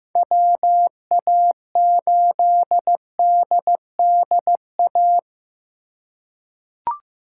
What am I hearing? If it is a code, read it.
Morse code, heard as a keyed tone, about 15 wpm, 700 Hz. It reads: WA8DDA
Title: USA Call Signs - 6 Characters - 2x Character Spacing - 15wpm